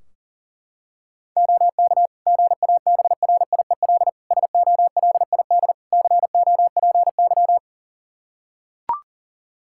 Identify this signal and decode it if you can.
Morse code, heard as a keyed tone, about 40 wpm, 690 Hz. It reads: OK GABRIEL SOLID COPY
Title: Practice QSO - Exchange 3 - More Info - 40wpm